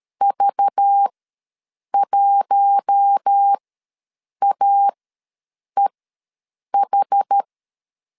Morse code, heard as a keyed tone, about 13 wpm, spaced as for 7 wpm, 800 Hz. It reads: V1AEH
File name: LEZIONI / 5 LBVC / 5 LBV ws 12wpm 3015